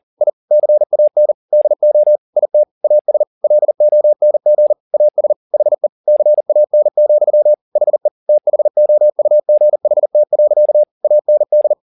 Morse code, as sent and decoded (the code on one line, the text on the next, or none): .. / -.-. .- -. / -.. --- / .. - / .- ... / .-.. --- -. --. / .- ... / .... / -.-. .- -. --..-- / .... / - .... --- ..- --. .... - .-.-.- / .- -. -..
I CAN DO IT AS LONG AS HE CAN, HE THOUGHT. AND